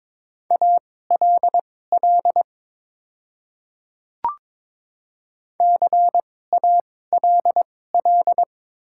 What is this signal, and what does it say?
Morse code, heard as a keyed tone, about 23 wpm, spaced as for 16 wpm, 700 Hz. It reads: ALL CALL
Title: 2x Character Spacing - Top 100 Words - 22wpm